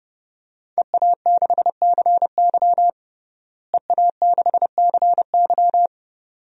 Morse code, heard as a keyed tone, about 30 wpm, 700 Hz. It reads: EA6CY EA6CY
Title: DXCC-100 Call Signs - 5 Characters - Repeated 2x - 30wpm